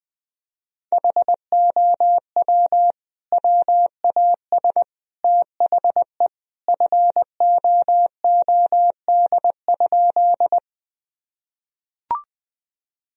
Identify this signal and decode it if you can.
Morse code, heard as a keyed tone, about 20 wpm, 700 Hz. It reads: HOW WAS THE FOOD?
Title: Sentences from Top 1000 Words - Review Part 1 - 20wpm